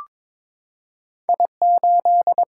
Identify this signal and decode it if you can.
Morse code, heard as a keyed tone, about 22 wpm, 700 Hz. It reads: I8
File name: practice / 22wpm / Two Letter-Number Combinations - 22wpm